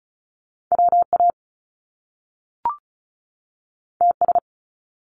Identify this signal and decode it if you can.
Morse code, heard as a keyed tone, about 36 wpm, 700 Hz. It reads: WA TS
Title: Top Two Letter Combinations in Words 35wpm